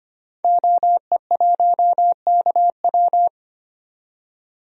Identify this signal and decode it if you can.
Morse code, heard as a keyed tone, about 25 wpm, 700 Hz. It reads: OE1KW
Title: DXCC-100 Call Signs - 5 Characters - 25wpm